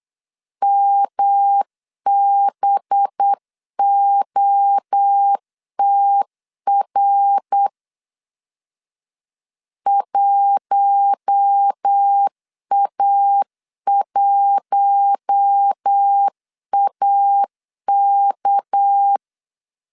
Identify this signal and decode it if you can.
Morse code, heard as a keyed tone, about 8 wpm, 800 Hz. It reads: MBOTR 1A1AK
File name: CW / DET / 1 6 1500C S C 8wpm 5219